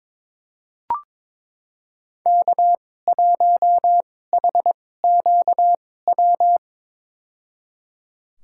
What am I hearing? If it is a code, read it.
Morse code, heard as a keyed tone, about 22 wpm, spaced as for 16 wpm, 700 Hz. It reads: K1HQW
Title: USA Call Signs - 5 Characters - 2x Character Spacing - 22wpm